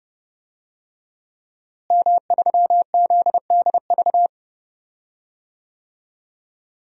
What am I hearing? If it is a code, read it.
Morse code, heard as a keyed tone, about 30 wpm, 700 Hz. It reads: M3ZDV
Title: DXCC-100 Call Signs - 5 Characters - Repeated 3x - 30wpm